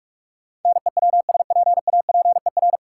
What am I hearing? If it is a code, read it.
Morse code, heard as a keyed tone, about 46 wpm, 700 Hz. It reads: NEWSPAPER
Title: Sentences from Top 1000 Words - Review Part 4 - New Word Drill - 45wpm